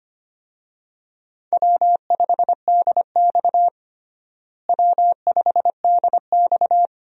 Morse code, heard as a keyed tone, about 25 wpm, 700 Hz. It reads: W5DX W5DX
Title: USA Call Signs - 4 Characters - Repeated 2x - 25wpm